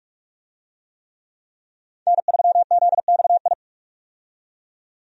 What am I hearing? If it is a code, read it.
Morse code, heard as a keyed tone, about 46 wpm, 700 Hz. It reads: N3ZXI
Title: USA Call Signs - 5 Characters - 45wpm